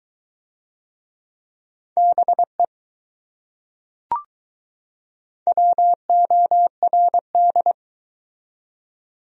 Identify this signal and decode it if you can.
Morse code, heard as a keyed tone, about 20 wpm, 700 Hz. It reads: BE WORD